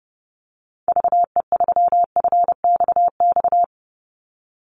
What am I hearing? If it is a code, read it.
Morse code, heard as a keyed tone, about 30 wpm, 700 Hz. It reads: VE3FXX
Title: Call Signs Repeated 3x - 30wpm